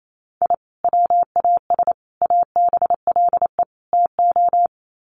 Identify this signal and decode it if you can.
Morse code, heard as a keyed tone, about 28 wpm, 700 Hz. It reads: I WAS ABLE TO
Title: Sets of 4 Words 28wpm